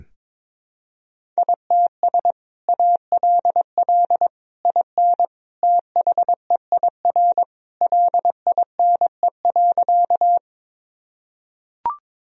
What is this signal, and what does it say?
Morse code, heard as a keyed tone, about 22 wpm, 700 Hz. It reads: ITS ALL IN THEIR LINE.
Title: Sentences from Top 1000 Words - Review Part 1 - 22wpm